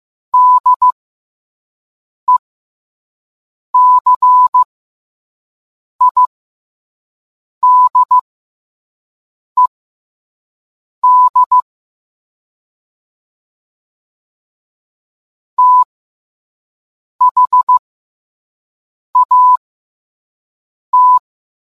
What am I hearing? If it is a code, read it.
Morse code, heard as a keyed tone, about 15 wpm, spaced as for 5 wpm, 1000 Hz. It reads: DECIDED THAT